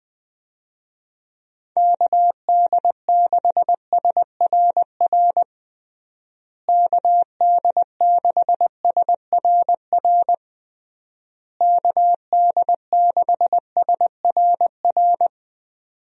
Morse code, heard as a keyed tone, about 20 wpm, 700 Hz. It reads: KD6SRR KD6SRR KD6SRR
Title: USA Call Signs - 6 Characters - Repeated 3x - 20wpm